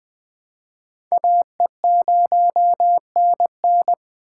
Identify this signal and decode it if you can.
Morse code, heard as a keyed tone, about 22 wpm, 700 Hz. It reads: AE0NN